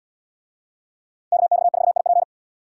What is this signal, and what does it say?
Morse code, heard as a keyed tone, about 71 wpm, 700 Hz. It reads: KX4IP